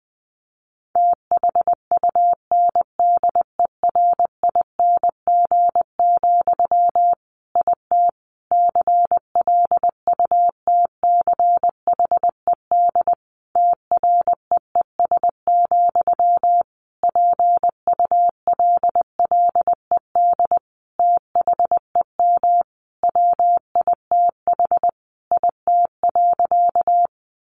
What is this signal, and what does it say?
Morse code, heard as a keyed tone, about 20 wpm, 700 Hz. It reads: THUNDERING, IT CLUTCHED TREES, PULLED THEM WITH IT.
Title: A Sound Of Thunder 20wpm